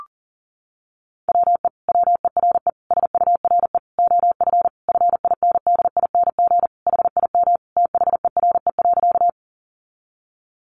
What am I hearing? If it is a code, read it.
Morse code, heard as a keyed tone, about 40 wpm, 700 Hz. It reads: WE WERE SURE OF FINDING HIM THERE.
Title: Sentences from Top 1100 Words 40wpm